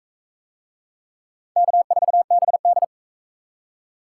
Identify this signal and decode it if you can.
Morse code, heard as a keyed tone, about 39 wpm, 700 Hz. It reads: K4BD